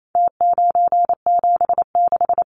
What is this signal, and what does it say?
Morse code, heard as a keyed tone, about 28 wpm, 700 Hz. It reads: T976
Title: Numbers - Contest Style - 4-Digit 28wpm